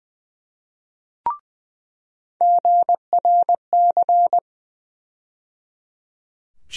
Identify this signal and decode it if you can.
Morse code, heard as a keyed tone, about 20 wpm, 700 Hz. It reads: GRC
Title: USA Call Sign Postfixes - 3 Letters 20wpm